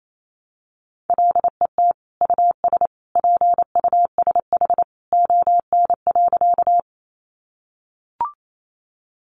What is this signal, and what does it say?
Morse code, heard as a keyed tone, about 28 wpm, 700 Hz. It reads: LET US PUSH ON.
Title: Sentences from Top 1300 Words 28wpm